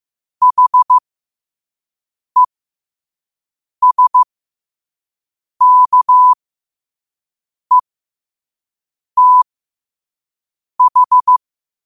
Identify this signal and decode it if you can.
Morse code, heard as a keyed tone, about 15 wpm, spaced as for 5 wpm, 1000 Hz. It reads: HESKETH
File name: qotdfiles / 5wpm